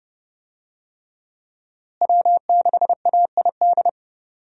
Morse code, heard as a keyed tone, about 30 wpm, 700 Hz. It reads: W6AID
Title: USA Call Signs - 5 Characters - Repeated 3x - 30wpm